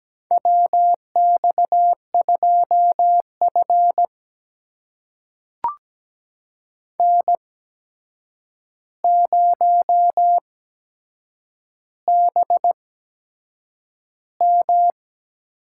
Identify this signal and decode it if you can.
Morse code, heard as a keyed tone, about 17 wpm, spaced as for 7 wpm, 700 Hz. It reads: WX2F N0BM
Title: USA Call Signs - 4 Characters - 8x Character Spacing - 17wpm